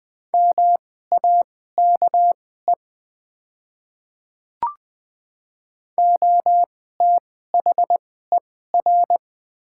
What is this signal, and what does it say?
Morse code, heard as a keyed tone, about 20 wpm, spaced as for 14 wpm, 700 Hz. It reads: MAKE OTHER
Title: 2x Character Spacing - Top 100 Words - 20wpm